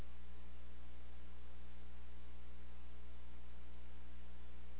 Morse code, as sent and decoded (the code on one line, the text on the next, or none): none